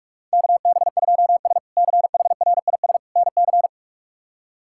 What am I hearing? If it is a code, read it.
Morse code, heard as a keyed tone, about 43 wpm, 700 Hz. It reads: KB2S CHRIS NC
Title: K1USN SST QSO Practice - 45wpm